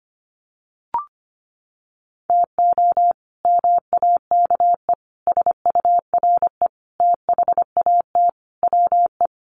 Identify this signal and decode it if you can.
Morse code, heard as a keyed tone, about 25 wpm, 700 Hz. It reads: TO MAKE SURE THAT WE
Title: Sets of 5 Words Encore 25wpm